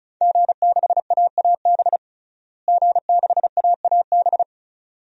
Morse code, heard as a keyed tone, about 34 wpm, 700 Hz. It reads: G6AAB G6AAB
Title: DXCC-100 Call Signs - 5 Characters - Repeated 2x - 35wpm